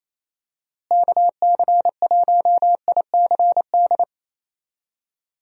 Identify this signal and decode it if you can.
Morse code, heard as a keyed tone, about 27 wpm, 700 Hz. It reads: KC1ICD
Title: USA Call Signs - 6 Characters - Repeated 2x - 28wpm